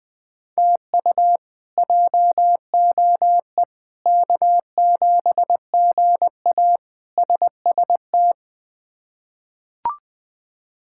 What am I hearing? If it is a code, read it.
Morse code, heard as a keyed tone, about 20 wpm, 700 Hz. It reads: TU JOE K7GA SST
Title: K1USN SST QSO Practice - 20wpm